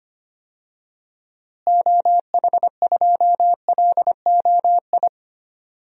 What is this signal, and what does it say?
Morse code, heard as a keyed tone, about 25 wpm, 700 Hz. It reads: OH2LOI
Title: DXCC-100 Call Signs - 6 Characters - Repeated 3x - 25wpm